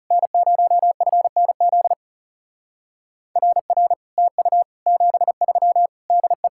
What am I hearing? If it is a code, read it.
Morse code, heard as a keyed tone, about 35 wpm, 700 Hz. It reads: N0FNZ RR TU 73 DE
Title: Speed-Racing Common QSO Phrases - 30wpm